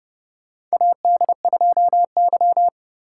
Morse code, heard as a keyed tone, about 30 wpm, 700 Hz. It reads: AD2Y
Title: USA Call Signs - 4 Characters - 3x Character Spacing - 30wpm